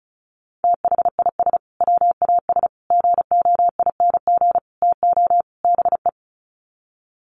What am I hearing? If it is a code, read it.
Morse code, heard as a keyed tone, about 34 wpm, 700 Hz. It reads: THIS WAS GOING TO BE
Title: Sets of 5 Words Encore2 35wpm